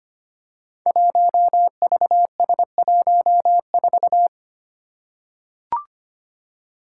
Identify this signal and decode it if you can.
Morse code, heard as a keyed tone, about 25 wpm, 700 Hz. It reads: 1VS14
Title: Five Challenging Letter-Number Combinations 25wpm